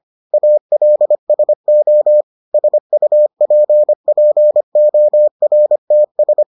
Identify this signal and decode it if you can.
Morse code, heard as a keyed tone, about 25 wpm, 600 Hz. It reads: ALSO SUPPORTS